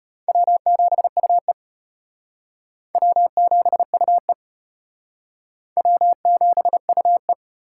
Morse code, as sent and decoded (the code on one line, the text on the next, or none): .-- --... ..- . / .-- --... ..- . / .-- --... ..- .
W7UE W7UE W7UE